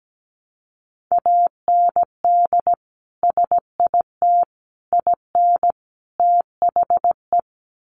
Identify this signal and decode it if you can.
Morse code, heard as a keyed tone, about 17 wpm, 700 Hz. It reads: AND SIT IN THE
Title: Sentences from Top 500 Words 17wpm